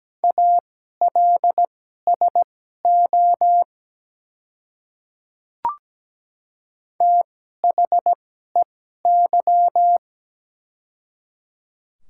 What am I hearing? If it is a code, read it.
Morse code, heard as a keyed tone, about 17 wpm, spaced as for 12 wpm, 700 Hz. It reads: ALSO THEY
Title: 2x Character Spacing - Top 300 Words - 17wpm